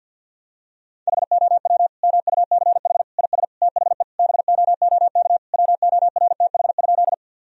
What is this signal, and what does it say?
Morse code, heard as a keyed tone, about 51 wpm, 700 Hz. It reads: HOW MUCH IS THE BOOK WORTH?
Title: Sentences from Top 1000 Words - Review Part 1 - 50wpm